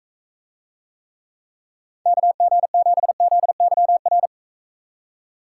Morse code, heard as a keyed tone, about 44 wpm, 700 Hz. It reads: KG7ZYR